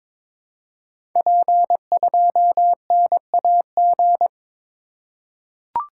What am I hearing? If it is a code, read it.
Morse code, heard as a keyed tone, about 22 wpm, 700 Hz. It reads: P2NAG
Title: Code Groups - 5 Letter:Numbers - 22wpm